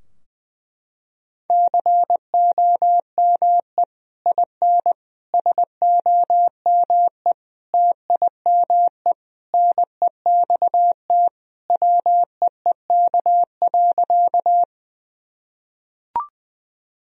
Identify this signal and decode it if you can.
Morse code, heard as a keyed tone, about 20 wpm, 700 Hz. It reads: COME IN SOME TIME NEXT WEEK.
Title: Sentences from Top 1000 Words - Review Part 2 - 20wpm